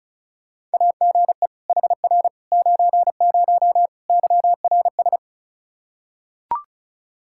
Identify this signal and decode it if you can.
Morse code, heard as a keyed tone, about 36 wpm, 700 Hz. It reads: AGE HR 90 YRS
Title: Practice QSO - Exchange 3 - More Info - 35wpm